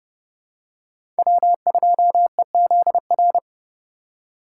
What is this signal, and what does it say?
Morse code, heard as a keyed tone, about 30 wpm, 700 Hz. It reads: W2EZR
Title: USA Call Signs - 5 Characters - 3x Character Spacing - 30wpm